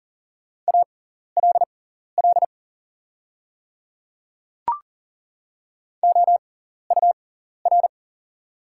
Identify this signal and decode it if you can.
Morse code, heard as a keyed tone, about 39 wpm, spaced as for 14 wpm, 700 Hz. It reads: ALL OUR
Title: 6x Character Spacing - Top 200 Words - 40wpm